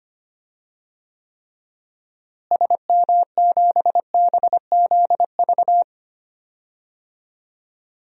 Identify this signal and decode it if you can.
Morse code, heard as a keyed tone, about 25 wpm, 700 Hz. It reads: SM7BZV
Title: DXCC-100 Call Signs - 6 Characters - Repeated 3x - 25wpm